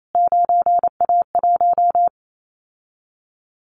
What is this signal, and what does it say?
Morse code, heard as a keyed tone, about 28 wpm, 700 Hz. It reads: 9A1
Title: Callsign Prefixes for Top 25 DXCC - Repeated 3x - 28wpm